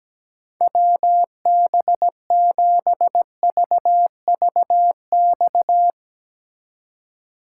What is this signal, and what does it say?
Morse code, heard as a keyed tone, about 18 wpm, 700 Hz. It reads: WB7VVX
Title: Speed Racing - USA 2x3 Call Signs 17wpm